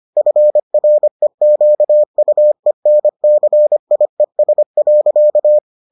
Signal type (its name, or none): Morse code